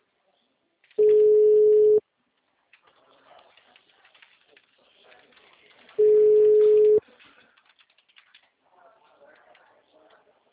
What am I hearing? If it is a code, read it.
Ringback tone (Europe, 425 Hz), with speech.
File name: out- -2004- - - .312